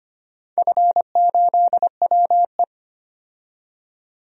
Morse code, heard as a keyed tone, about 25 wpm, 700 Hz. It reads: F8WE
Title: DXCC-100 Call Signs - 4 Characters - 25wpm